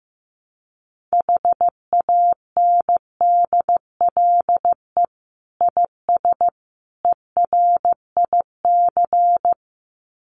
Morse code, heard as a keyed tone, about 15 wpm, 700 Hz. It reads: HANDLE IS ERIC
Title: Common QSO Phrases 15wpm